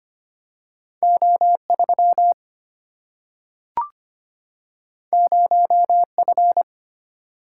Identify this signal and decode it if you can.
Morse code, heard as a keyed tone, about 25 wpm, 700 Hz. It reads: O 3 0 F
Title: Single-Letter Single-Number 25wpm